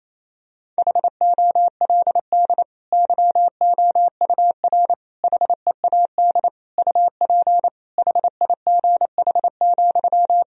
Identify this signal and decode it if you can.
Morse code, heard as a keyed tone, about 27 wpm, 700 Hz. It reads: HOLD YOUR HEAD UP HIGH,
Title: Sentences from Top 1000 Words - Review Part 2 - 28wpm